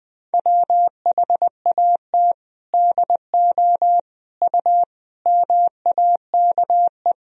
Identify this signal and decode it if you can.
Morse code, heard as a keyed tone, about 21 wpm, 700 Hz. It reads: WHAT DO U MAKE
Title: Sentences from Top 100 Words and Abbreviations 20wpm